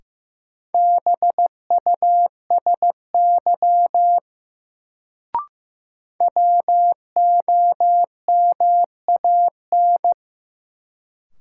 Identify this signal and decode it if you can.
Morse code, heard as a keyed tone, about 15 wpm, 700 Hz. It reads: BUSY WOMAN
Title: Sentences from Top 1000 Words - Review Part 2 - New Word Drill - 15wpm